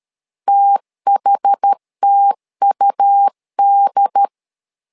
Morse code, heard as a keyed tone, about 13 wpm, 800 Hz. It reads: THTUD
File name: LEZIONI / 2 UHDN / 2 UHD 12wpm 1540